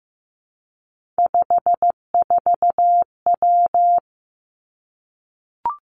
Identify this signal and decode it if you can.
Morse code, heard as a keyed tone, about 15 wpm, 700 Hz. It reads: 54W
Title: Three Challenging Letter-Number Combinations 15wpm